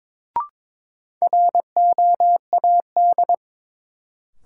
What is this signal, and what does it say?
Morse code, heard as a keyed tone, about 21 wpm, 700 Hz. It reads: ROAD
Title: Sets of 5 Words - New Word Drill - 22wpm